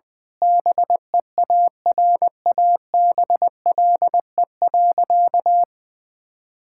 Morse code, heard as a keyed tone, about 20 wpm, 700 Hz. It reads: BEARABLE.
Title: Difficult Sentences with Hints - 20wpm